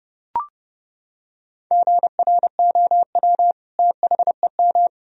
Morse code, heard as a keyed tone, about 30 wpm, 700 Hz. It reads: GROW THEM
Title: Sentences from Top 1000 Words - Review Part 1 - 30wpm